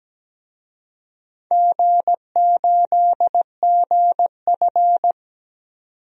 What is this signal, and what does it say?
Morse code, heard as a keyed tone, about 17 wpm, 700 Hz. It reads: G8GF
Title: DXCC-100 Call Signs - 4 Characters - 17wpm